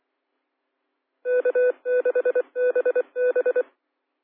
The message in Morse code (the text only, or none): K6BB